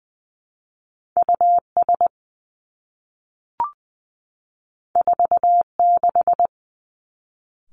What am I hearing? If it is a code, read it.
Morse code, heard as a keyed tone, about 19 wpm, 700 Hz. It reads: U S 4 6
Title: Two Challenging Letter-Number Combinations 20wpm